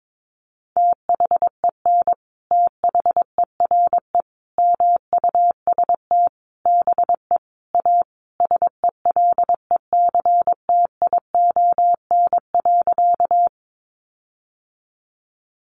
Morse code, heard as a keyed tone, about 22 wpm, 700 Hz. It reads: THEN THERE MUST BE A SELECTION.
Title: Sentences from Top 1400 Words 22wpm